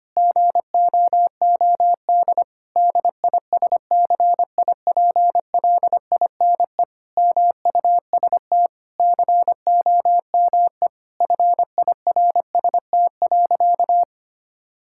Morse code, heard as a keyed tone, about 25 wpm, 700 Hz. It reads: GOOD DISCIPLINE MUST COME FIRST.
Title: Sentences from Top 1000 Words - Review Part 6 - 25wpm